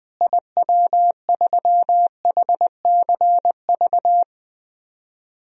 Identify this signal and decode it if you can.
Morse code, heard as a keyed tone, about 20 wpm, 700 Hz. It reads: IW3HCV
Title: DXCC-100 Call Signs - 6 Characters - Repeated 2x - 20wpm